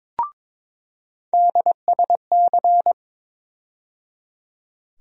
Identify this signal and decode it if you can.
Morse code, heard as a keyed tone, about 22 wpm, 700 Hz. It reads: DSC
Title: USA Call Sign Postfixes - 3 Letters 22wpm